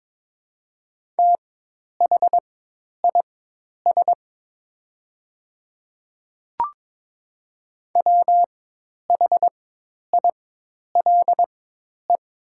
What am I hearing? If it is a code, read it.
Morse code, heard as a keyed tone, about 22 wpm, spaced as for 10 wpm, 700 Hz. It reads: THIS WHILE
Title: 4x Character Spacing - Top 300 Words - 22wpm